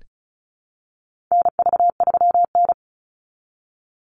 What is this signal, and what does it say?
Morse code, heard as a keyed tone, about 34 wpm, 700 Hz. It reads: NV3N
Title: Call Signs - 35wpm